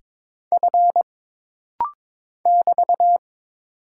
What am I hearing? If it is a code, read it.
Morse code, heard as a keyed tone, about 22 wpm, 700 Hz. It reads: F <BT>
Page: This rapid-fire timing is geared to push your ICR (Instant Character Recognition) skills.